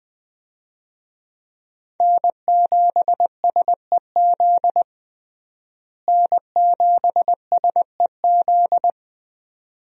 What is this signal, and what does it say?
Morse code, heard as a keyed tone, about 20 wpm, 700 Hz. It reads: N7SEZ N7SEZ
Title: USA Call Signs - 5 Characters - Repeated 2x - 20wpm